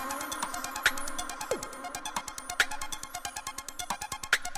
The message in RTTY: none